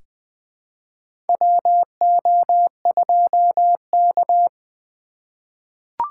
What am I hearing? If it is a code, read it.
Morse code, heard as a keyed tone, about 21 wpm, 700 Hz. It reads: WO2K
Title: USA Call Signs - 4 Characters - 3x Character Spacing - 20wpm